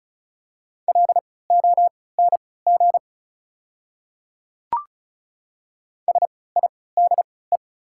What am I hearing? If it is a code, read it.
Morse code, heard as a keyed tone, about 34 wpm, spaced as for 20 wpm, 690 Hz. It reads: LONG SIDE
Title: 3x Character Spacing - Top 100 Words - 35wpm